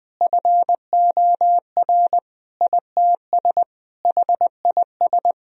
Morse code, heard as a keyed tone, about 20 wpm, 700 Hz. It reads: FOR ITS HIS
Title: Sentences from Top 1000 Words - Review Part 2 - 20wpm